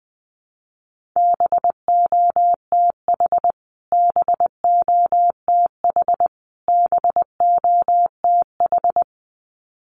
Morse code, heard as a keyed tone, about 20 wpm, 700 Hz. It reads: BOTH BOTH BOTH
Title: Top 500 Words Repeated 20wpm